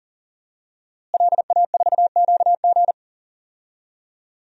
Morse code, heard as a keyed tone, about 40 wpm, 700 Hz. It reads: LA4QG
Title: DXCC-100 Call Signs - 5 Characters - Repeated 2x - 40wpm